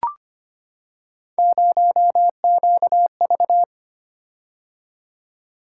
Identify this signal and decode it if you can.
Morse code, heard as a keyed tone, about 25 wpm, 700 Hz. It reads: 0QV
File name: Three Letter-Number Combinations - 25wpm